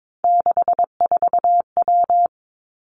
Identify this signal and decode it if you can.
Morse code, heard as a keyed tone, about 22 wpm, 700 Hz. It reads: 64W
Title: Three Challenging Letter-Number Combinations 22wpm